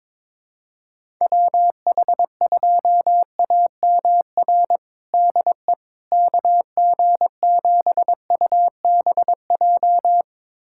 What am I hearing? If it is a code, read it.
Morse code, heard as a keyed tone, about 22 wpm, 700 Hz. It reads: WH2AMR DE KG7UBJ
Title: USA Callsign DE Callsign - 6 Characters - 22wpm